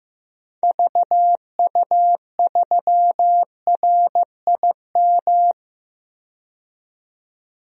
Morse code, heard as a keyed tone, about 15 wpm, 700 Hz. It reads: VU3RIM
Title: DXCC-100 Call Signs - 6 Characters - 15wpm